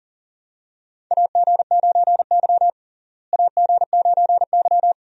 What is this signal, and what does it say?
Morse code, heard as a keyed tone, about 40 wpm, 700 Hz. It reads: AG9Y AG9Y
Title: USA Call Signs - 4 Characters - Repeated 2x - 40wpm